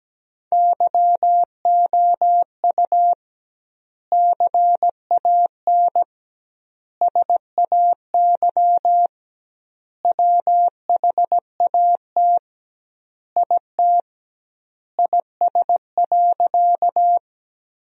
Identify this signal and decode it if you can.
Morse code, heard as a keyed tone, about 17 wpm, 700 Hz. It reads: YOU CAN SAY WHAT IT IS.
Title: 2x Word-Spacing - Sentences from Top 200 Words - 17wpm